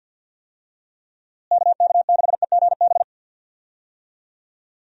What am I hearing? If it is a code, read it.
Morse code, heard as a keyed tone, about 47 wpm, 700 Hz. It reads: KK6EGB
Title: USA Call Signs - 6 Characters - Repeated 3x - 50wpm